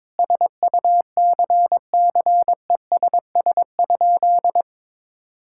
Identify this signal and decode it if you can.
Morse code, heard as a keyed tone, about 21 wpm, 700 Hz. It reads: SUCCESS?
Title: Sentences from Top 1000 Words - Review Part 6 - 22wpm